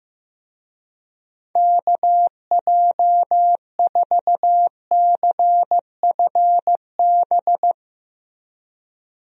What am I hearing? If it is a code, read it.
Morse code, heard as a keyed tone, about 15 wpm, 700 Hz. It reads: KJ4CFB
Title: USA Call Signs - 6 Characters - Repeated 2x - 15wpm